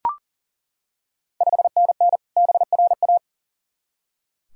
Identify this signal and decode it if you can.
Morse code, heard as a keyed tone, about 40 wpm, 690 Hz. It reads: 5NN BRA
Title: Ontario QSO Party Exchange 40wpm